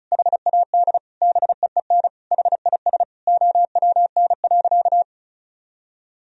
Morse code, heard as a keyed tone, about 34 wpm, 700 Hz. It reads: HAD BEEN HIS OWN.
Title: Sentences from Top 1000 Words - Review Part 2 - 35wpm